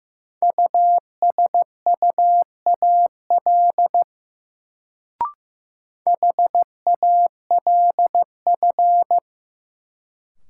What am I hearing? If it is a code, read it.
Morse code, heard as a keyed tone, about 15 wpm, 700 Hz. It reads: USUAL HALF
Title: Sentences from Top 400 Words - New Word Drill - 15wpm